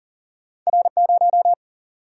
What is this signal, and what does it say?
Morse code, heard as a keyed tone, about 39 wpm, 700 Hz. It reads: R0
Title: Single-Letter Single-Number 40wpm